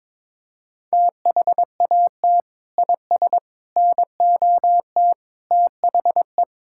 Morse code, heard as a keyed tone, about 22 wpm, 700 Hz. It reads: THAT IS NOT THE
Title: Sentences from Top 1000 Words - Review Part 5 - 22wpm